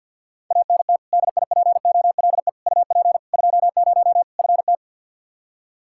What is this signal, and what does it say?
Morse code, heard as a keyed tone, about 48 wpm, 700 Hz. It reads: ANT DIPOLE UP 20 FT
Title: Practice QSO - Exchange 2 - Info - 50wpm